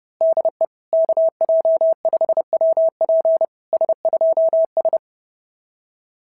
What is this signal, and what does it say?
Morse code, heard as a keyed tone, about 32 wpm, 690 Hz. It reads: DE KJ5WP S2S
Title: SOTA QSO Practice 30wpm